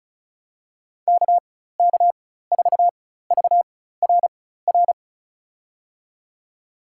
Morse code, heard as a keyed tone, about 36 wpm, spaced as for 16 wpm, 700 Hz. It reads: KK4VRR
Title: USA Call Signs - 6 Characters - 4x Character Spacing - 35wpm